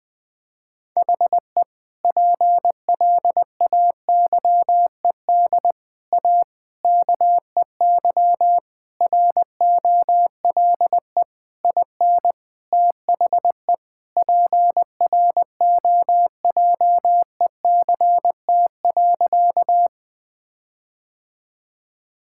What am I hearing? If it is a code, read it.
Morse code, heard as a keyed tone, about 20 wpm, 700 Hz. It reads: HE PLAYED A KEY ROLE IN THE PROJECT.
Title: Sentences with 6-Word Phrases - 20wpm